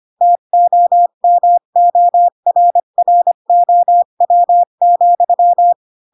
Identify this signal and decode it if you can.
Morse code, heard as a keyed tone, about 25 wpm, 700 Hz. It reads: TOMORROW,
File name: Pinocchio / Pinocchio ch4 20 25 ik1hge